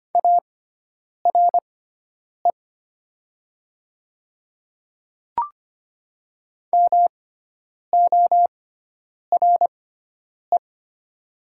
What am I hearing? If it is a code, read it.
Morse code, heard as a keyed tone, about 25 wpm, spaced as for 9 wpm, 700 Hz. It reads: ARE MORE
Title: 6x Character Spacing - Top 200 Words - 25wpm